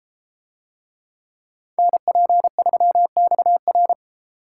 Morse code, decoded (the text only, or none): NP3XR